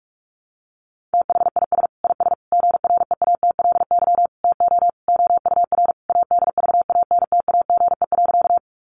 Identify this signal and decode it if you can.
Morse code, heard as a keyed tone, about 44 wpm, 700 Hz. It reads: THIS IS GREATLY TO OUR ADVANTAGE.
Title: Sentences from Top 1400 Words 45wpm